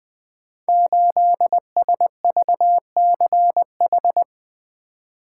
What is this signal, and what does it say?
Morse code, heard as a keyed tone, about 20 wpm, 700 Hz. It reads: 8SVCH
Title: Code Groups - 5 Letter:Numbers - 20wpm